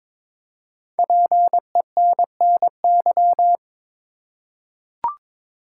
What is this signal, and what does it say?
Morse code, heard as a keyed tone, about 22 wpm, 700 Hz. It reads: PENNY
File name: 5 Letter Words - First 3 Letters Given - 22wpm